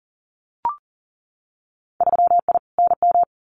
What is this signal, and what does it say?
Morse code, heard as a keyed tone, about 39 wpm, 700 Hz. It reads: 3I NM
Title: Winter Field Day Exchanges Encore 40wpm